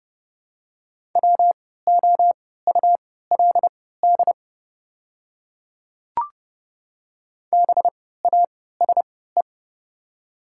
Morse code, heard as a keyed tone, about 30 wpm, spaced as for 17 wpm, 700 Hz. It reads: WOULD BASE
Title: 3x Character Spacing - Top 300 Words - 30wpm